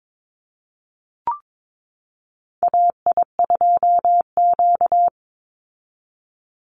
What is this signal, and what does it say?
Morse code, heard as a keyed tone, about 22 wpm, 700 Hz. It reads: AI2Q
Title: Call Signs Repeated 3x - 22wpm